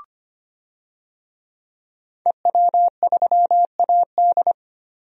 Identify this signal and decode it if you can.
Morse code, heard as a keyed tone, about 25 wpm, 700 Hz. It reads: EW3AD
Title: DXCC-100 Call Signs - 5 Characters - 25wpm